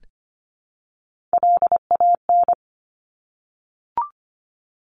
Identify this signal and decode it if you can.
Morse code, heard as a keyed tone, about 25 wpm, 700 Hz. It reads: LAN
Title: Top Three Letter Combinations in Words 25wpm